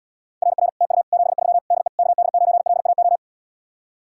Morse code, heard as a keyed tone, about 80 wpm, 700 Hz. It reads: RR TU 73 DE KD9LAO